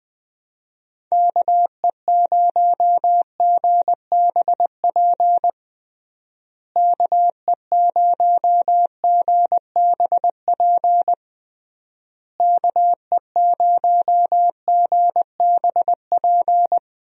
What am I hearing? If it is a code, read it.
Morse code, heard as a keyed tone, about 20 wpm, 700 Hz. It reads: KE0GBP KE0GBP KE0GBP
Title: USA Call Signs - 6 Characters - Repeated 3x - 20wpm